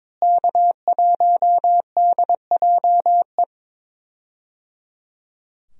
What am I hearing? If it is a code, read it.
Morse code, heard as a keyed tone, about 22 wpm, 700 Hz. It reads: K1DJE
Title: USA Call Signs - 5 Characters - Repeated 3x - 22wpm